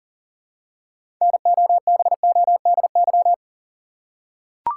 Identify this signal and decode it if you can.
Morse code, heard as a keyed tone, about 40 wpm, 700 Hz. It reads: NOBODY
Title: Six Letter Words - 40wpm